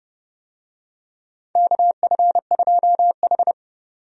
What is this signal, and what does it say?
Morse code, heard as a keyed tone, about 31 wpm, 700 Hz. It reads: KF2H
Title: USA Call Signs - 4 Characters - 30wpm